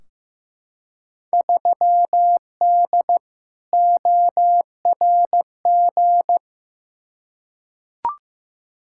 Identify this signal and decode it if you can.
Morse code, heard as a keyed tone, about 15 wpm, 700 Hz. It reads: 3D ORG
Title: Field Day Exchanges Encore - 15wpm